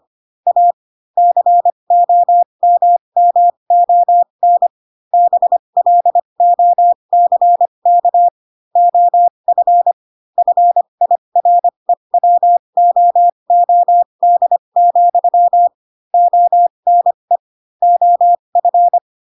Morse code, heard as a keyed tone, about 25 wpm, spaced as for 22 wpm, 700 Hz. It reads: A COMMON BLOCK OF FIREWOOD, ONE OF